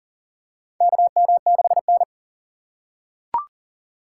Morse code, heard as a keyed tone, about 40 wpm, 700 Hz. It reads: KM6N
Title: USA Call Signs - 4 Characters - 4x Character Spacing - 40wpm